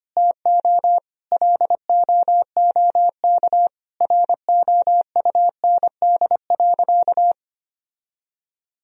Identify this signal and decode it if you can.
Morse code, heard as a keyed tone, about 25 wpm, 700 Hz. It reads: TO LOOK ROUND.
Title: Sentences from Top 1000 Words - Review Part 1 - 25wpm